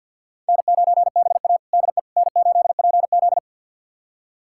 Morse code, heard as a keyed tone, about 46 wpm, 700 Hz. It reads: N9BA DE N8PZ
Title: USA Callsign DE Callsign - 4 Characters - 50wpm